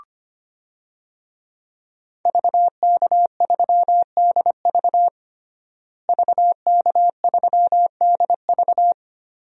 Morse code, heard as a keyed tone, about 25 wpm, 700 Hz. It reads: VK3DV VK3DV
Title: DXCC-100 Call Signs - 5 Characters - Repeated 2x - 25wpm